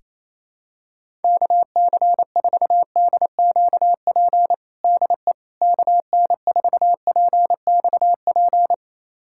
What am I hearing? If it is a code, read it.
Morse code, heard as a keyed tone, about 28 wpm, 700 Hz. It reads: KC4DQP DE KN4PXP
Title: USA Callsign DE Callsign 28wpm